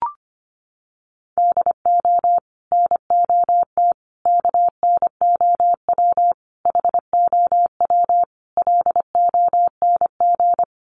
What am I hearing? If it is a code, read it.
Morse code, heard as a keyed tone, about 25 wpm, 700 Hz. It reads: DO NOT KNOW HOW LONG
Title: Sets of 5 Words Encore2 25wpm